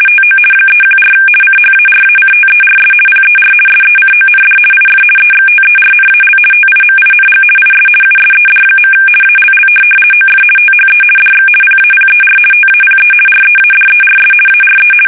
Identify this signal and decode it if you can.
RTTY, about 50 baud. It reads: TO 314 CHARACTERS PER SECOND IF EXTERNAL TIMING IS USED. ALL OF THE DX PAPER TAPE READERS ARE
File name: fsk850-50B / teletype-corp